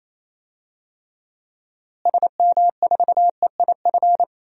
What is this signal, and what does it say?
Morse code, heard as a keyed tone, about 28 wpm, 700 Hz. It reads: SM4EIF
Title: DXCC-100 Call Signs - 6 Characters - Repeated 3x - 28wpm